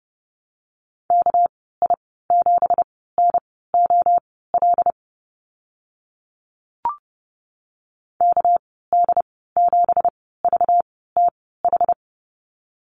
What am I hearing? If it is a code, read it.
Morse code, heard as a keyed tone, about 30 wpm, spaced as for 17 wpm, 700 Hz. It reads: KI7NOL KD7VTH
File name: Lesson 37g - Letter x - Call Signs 30wpm